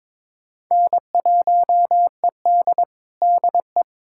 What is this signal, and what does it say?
Morse code, heard as a keyed tone, about 22 wpm, 700 Hz. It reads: N1ED DE
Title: USA Callsign DE Callsign - 4 Characters - 22wpm